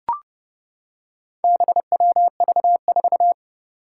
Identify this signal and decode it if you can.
Morse code, heard as a keyed tone, about 30 wpm, 700 Hz. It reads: BWV4
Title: Four Challenging Letter-Number Combinations 30wpm